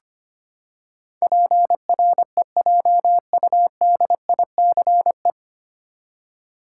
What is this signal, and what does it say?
Morse code, heard as a keyed tone, about 25 wpm, 700 Hz. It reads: PREJUDICE
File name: Nine Letter Words - 25wpm